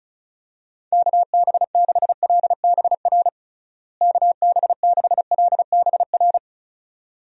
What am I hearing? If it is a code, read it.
Morse code, heard as a keyed tone, about 36 wpm, 700 Hz. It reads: KB6LBR KB6LBR
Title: USA Call Signs - 6 Characters - Repeated 2x - 35wpm